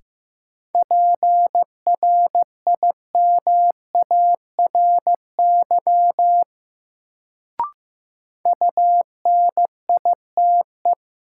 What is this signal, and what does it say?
Morse code, heard as a keyed tone, about 15 wpm, 700 Hz. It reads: PRIMARY UNITE
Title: Sentences from Top 1000 Words - Review Part 5 - New Word Drill - 15wpm